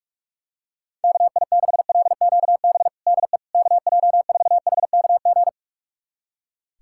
Morse code, heard as a keyed tone, about 46 wpm, 700 Hz. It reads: KI6LQB DE KJ4HKG
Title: USA Callsign DE Callsign - 6 Characters - 45wpm